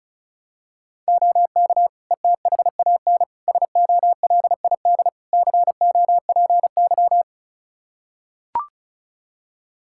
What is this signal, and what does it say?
Morse code, heard as a keyed tone, about 36 wpm, 700 Hz. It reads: OK ETHAN SOLID COPY
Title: Practice QSO - Exchange 3 - More Info - 35wpm